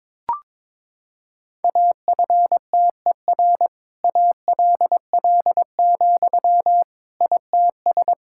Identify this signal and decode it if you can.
Morse code, heard as a keyed tone, about 22 wpm, 700 Hz. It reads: AFTER ALL, ITS
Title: Sentences from Top 1000 Words - Review Part 5 - 22wpm